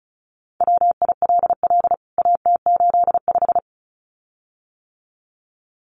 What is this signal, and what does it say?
Morse code, heard as a keyed tone, about 34 wpm, 700 Hz. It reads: WILL AT85
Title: CWT Exchange 35wpm